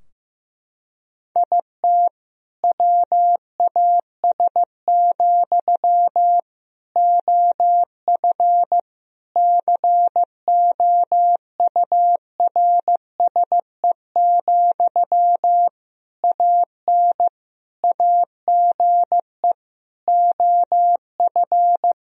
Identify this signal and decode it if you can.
Morse code, heard as a keyed tone, about 16 wpm, 700 Hz. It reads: IT WAS, OF COURSE, AN AGE OF
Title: Sentences from Top 1000 Words - Review Part 3 - 15wpm